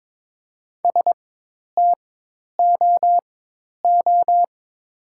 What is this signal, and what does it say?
Morse code, heard as a keyed tone, about 21 wpm, spaced as for 10 wpm, 700 Hz. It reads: STOO
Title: 4x Character Spacing - Top 500 Words - 22wpm